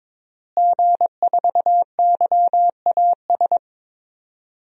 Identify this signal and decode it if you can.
Morse code, heard as a keyed tone, about 22 wpm, 700 Hz. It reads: G4YAS
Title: DXCC-100 Call Signs - 5 Characters - Repeated 2x - 22wpm